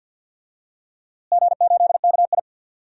Morse code, heard as a keyed tone, about 52 wpm, 700 Hz. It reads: G8KI
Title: DXCC-100 Call Signs - 4 Characters - Repeated 2x - 50wpm